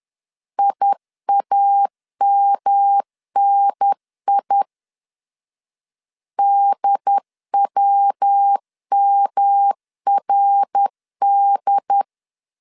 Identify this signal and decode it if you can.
Morse code, heard as a keyed tone, about 10 wpm, 800 Hz. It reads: IAMNI DWMRD
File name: CW / DET / 1 3 1500C S B 10wpm 3335